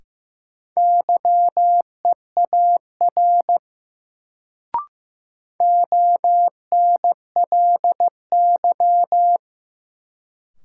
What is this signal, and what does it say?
Morse code, heard as a keyed tone, about 15 wpm, 700 Hz. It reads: YEAR ONLY
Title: Sentences from Top 200 Words - New Word Drill - 15wpm